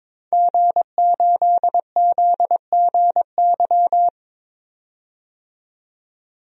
Morse code, heard as a keyed tone, about 22 wpm, 700 Hz. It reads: G8ZGY
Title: DXCC-100 Call Signs - 5 Characters - 22wpm